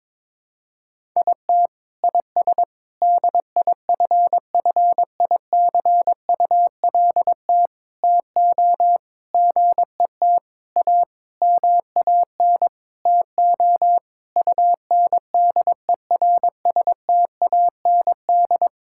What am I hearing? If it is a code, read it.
Morse code, heard as a keyed tone, about 22 wpm, 700 Hz. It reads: IT IS DIFFICULT TO GET A MAN TO UNDERSTAND
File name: Sentences with 9-Word Phrases - 22wpm